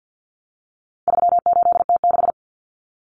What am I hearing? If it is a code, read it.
Morse code, heard as a keyed tone, about 50 wpm, 700 Hz. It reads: 38T6
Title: Common Cut Numbers - 4-Digit 50wpm